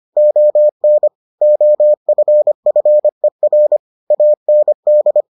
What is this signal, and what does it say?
Morse code, heard as a keyed tone, about 25 wpm, 600 Hz. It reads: ON OFFER AND